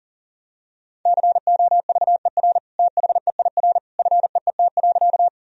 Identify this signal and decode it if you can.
Morse code, heard as a keyed tone, about 40 wpm, 700 Hz. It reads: COVER THEIR FEET.
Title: Sentences from Top 1000 Words - Review Part 1 - 40wpm